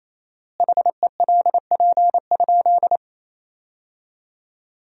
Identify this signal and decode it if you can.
Morse code, heard as a keyed tone, about 28 wpm, 700 Hz. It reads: HELP?
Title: 4x Word-Spacing - Sentences from Top 200 Words - 28wpm